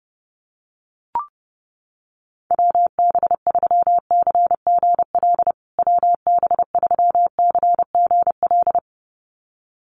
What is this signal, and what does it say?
Morse code, heard as a keyed tone, about 30 wpm, 700 Hz. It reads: WB3CGL WB3CGL
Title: USA Call Signs Repeated 2x 30wpm